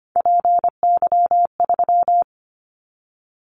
Morse code, heard as a keyed tone, about 25 wpm, 700 Hz. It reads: PY3
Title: Callsign Prefixes for Top 50 DXCC - 25wpm